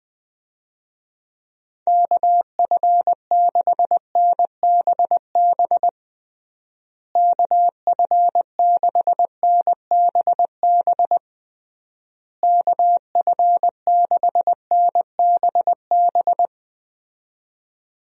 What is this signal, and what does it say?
Morse code, heard as a keyed tone, about 20 wpm, 700 Hz. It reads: KF6NBB KF6NBB KF6NBB
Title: USA Call Signs - 6 Characters - Repeated 3x - 20wpm